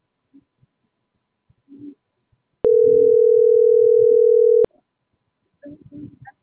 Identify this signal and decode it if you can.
Ringback tone (North America), 440 + 480 Hz.